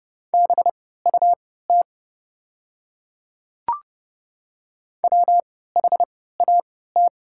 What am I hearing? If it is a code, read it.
Morse code, heard as a keyed tone, about 30 wpm, spaced as for 17 wpm, 700 Hz. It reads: BUT WHAT
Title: 3x Character Spacing - Top 100 Words - 30wpm